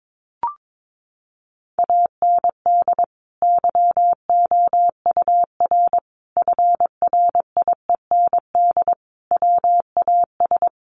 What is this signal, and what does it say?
Morse code, heard as a keyed tone, about 22 wpm, 700 Hz. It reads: AND YOUR FRIEND WAS
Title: Sentences from Top 1400 Words 22wpm